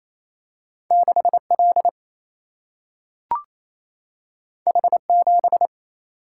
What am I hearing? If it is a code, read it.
Morse code, heard as a keyed tone, about 28 wpm, 700 Hz. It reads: 6 L H 7
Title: Single-Letter Single-Number 28wpm